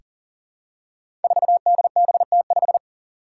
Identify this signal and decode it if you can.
Morse code, heard as a keyed tone, about 39 wpm, 700 Hz. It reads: 4DBT5
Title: Five Challenging Letter-Number Combinations 40wpm